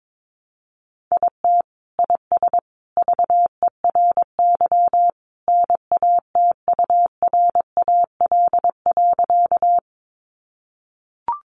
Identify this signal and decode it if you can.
Morse code, heard as a keyed tone, about 22 wpm, 700 Hz. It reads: IT IS VERY NATURAL.
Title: Sentences from Top 700 Words 22wpm